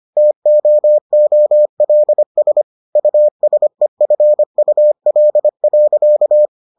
Morse code, heard as a keyed tone, about 25 wpm, 600 Hz. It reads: TOOLS USEFUL.